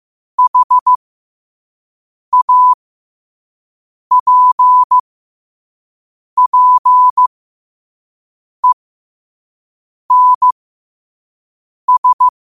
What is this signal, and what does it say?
Morse code, heard as a keyed tone, about 15 wpm, spaced as for 5 wpm, 1000 Hz. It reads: HAPPENS
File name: qotdfiles / 5wpm